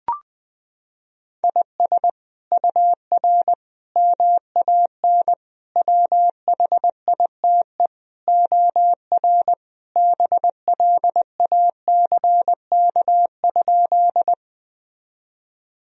Morse code, heard as a keyed tone, about 20 wpm, 700 Hz. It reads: IS UR MAN WHITE OR BLACK?
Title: Sentences from Top 400 Words and Abbreviations 20wpm